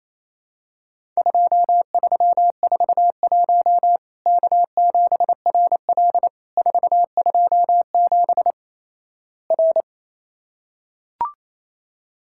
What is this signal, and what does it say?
Morse code, heard as a keyed tone, about 28 wpm, 700 Hz. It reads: 2341 K7RL 427 R
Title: Worked All Europe DX Contest - QTC Exchange - 28wpm